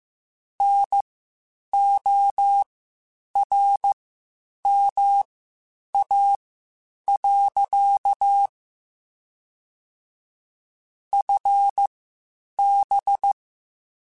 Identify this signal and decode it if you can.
Morse code, heard as a keyed tone, about 15 wpm, spaced as for 8 wpm, 800 Hz. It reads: NORMA. FB
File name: QSOs / QSOs-mp3s / WD0FBT 1@13